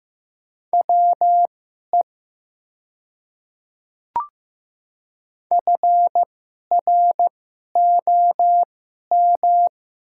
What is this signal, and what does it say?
Morse code, heard as a keyed tone, about 15 wpm, spaced as for 11 wpm, 700 Hz. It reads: WE FROM
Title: 2x Character Spacing - Top 100 Words - 15wpm